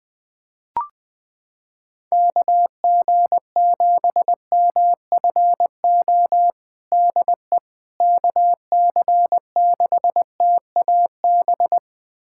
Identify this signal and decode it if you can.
Morse code, heard as a keyed tone, about 21 wpm, 700 Hz. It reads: KG7MFO DE KC6TAB
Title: USA Callsign DE Callsign 20wpm